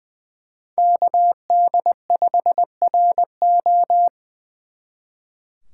Morse code, heard as a keyed tone, about 26 wpm, 700 Hz. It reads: KD5RO